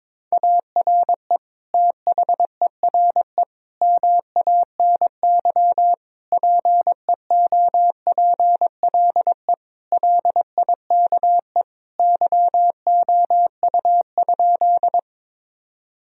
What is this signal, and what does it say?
Morse code, heard as a keyed tone, about 22 wpm, 700 Hz. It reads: ARE THERE MANY PEOPLE LIKE YOU?
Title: Sentences from Top 1000 Words - Review Part 1 - 22wpm